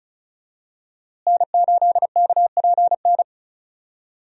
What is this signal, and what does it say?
Morse code, heard as a keyed tone, about 36 wpm, 700 Hz. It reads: N8KPN
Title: USA Call Signs - 5 Characters - Repeated 3x - 35wpm